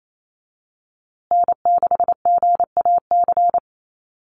Morse code, heard as a keyed tone, about 28 wpm, 700 Hz. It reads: N6GAC
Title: USA Call Signs Repeated 3x 28wpm